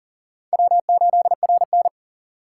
Morse code, heard as a keyed tone, about 41 wpm, 700 Hz. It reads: W8RN
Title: USA Call Signs - 4 Characters - 6x Character Spacing - 40wpm